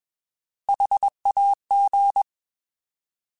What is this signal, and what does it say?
Morse code, heard as a keyed tone, about 21 wpm, 800 Hz. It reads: HAG
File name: Alphabetical / Alphabetical mp3s / Hs @20